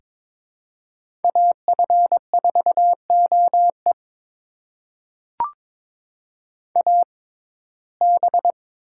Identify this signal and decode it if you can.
Morse code, heard as a keyed tone, about 22 wpm, spaced as for 16 wpm, 700 Hz. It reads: AF4OE AB
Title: USA Call Signs - 5 Characters - 6x Character Spacing - 22wpm